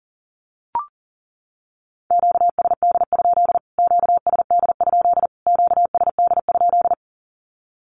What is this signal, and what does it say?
Morse code, heard as a keyed tone, about 40 wpm, 700 Hz. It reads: QSD? QSD? QSD?
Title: Q-Codes Repeated 3x 40wpm